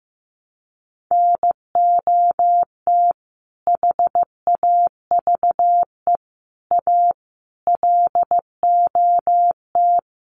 Morse code, heard as a keyed tone, about 15 wpm, 700 Hz. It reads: NOT HAVE A LOT
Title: Sets of 5 Words 15wpm